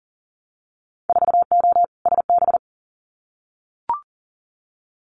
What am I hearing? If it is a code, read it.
Morse code, heard as a keyed tone, about 40 wpm, 700 Hz. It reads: 4O SB
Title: Winter Field Day Exchanges 40wpm